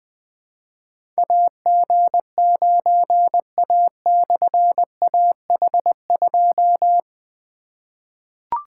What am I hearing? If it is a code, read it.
Morse code, heard as a keyed tone, about 20 wpm, 700 Hz. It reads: AG9A/AH2
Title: Difficult Call Signs 20wpm